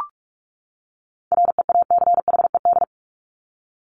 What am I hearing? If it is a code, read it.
Morse code, heard as a keyed tone, about 44 wpm, 700 Hz. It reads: REACHED
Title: ED Suffix - Top 100 Words with Base Words 45wpm